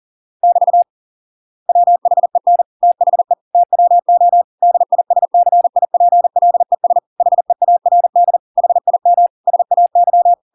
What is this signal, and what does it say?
Morse code, heard as a keyed tone, about 40 wpm, 700 Hz. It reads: <BT> WHEN THE TWO DISCIPLES HEARD HIM SAY